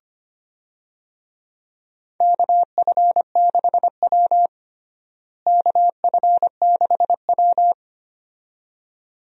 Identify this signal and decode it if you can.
Morse code, heard as a keyed tone, about 25 wpm, 700 Hz. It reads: KF6W KF6W
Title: USA Call Signs - 4 Characters - Repeated 2x - 25wpm